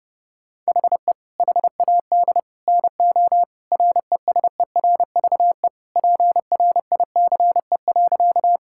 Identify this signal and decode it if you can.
Morse code, heard as a keyed tone, about 32 wpm, 700 Hz. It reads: HE HAD NO RESERVE PRICE.
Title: Sentences from Top 1000 Words - Review Part 6 - 30wpm